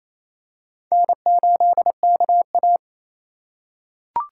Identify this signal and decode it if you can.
Morse code, heard as a keyed tone, about 28 wpm, 700 Hz. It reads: N8KA
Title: USA Call Signs - 4 Characters - 4x Character Spacing - 28wpm